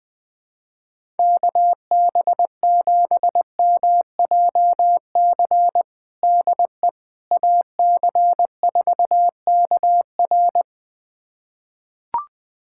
Morse code, heard as a keyed tone, about 21 wpm, 700 Hz. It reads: KB7MJC DE AC4KR
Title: USA Callsign DE Callsign 20wpm